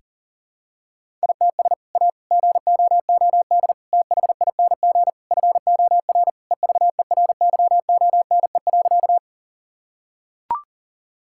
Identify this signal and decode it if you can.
Morse code, heard as a keyed tone, about 37 wpm, 700 Hz. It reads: ITS A GOOD THING FOR EVERYONE.
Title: Sentences from Top 1000 Words - Review Part 5 - 40wpm